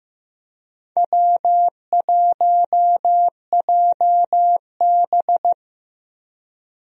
Morse code, heard as a keyed tone, about 15 wpm, 700 Hz. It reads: W1JB
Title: USA Call Signs - 4 Characters - Repeated 2x - 15wpm